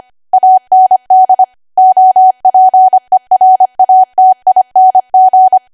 Morse code, heard as a keyed tone, about 25 wpm, 750 Hz. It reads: AND OPERATING
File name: Archive / 25 WPM / 25WPM